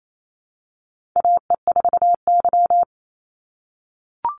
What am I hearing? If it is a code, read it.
Morse code, heard as a keyed tone, about 28 wpm, 700 Hz. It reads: AE4Y
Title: NA SOTA Chaser Callsigns 28wpm